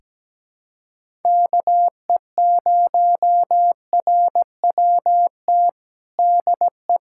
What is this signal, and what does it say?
Morse code, heard as a keyed tone, about 17 wpm, 700 Hz. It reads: KE0RWT DE
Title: USA Callsign DE Callsign - 6 Characters - 17wpm